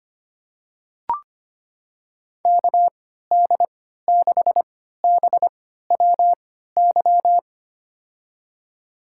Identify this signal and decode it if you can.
Morse code, heard as a keyed tone, about 25 wpm, spaced as for 14 wpm, 700 Hz. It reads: KD6BWY
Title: USA Call Signs - 3x Character Spacing - 25wpm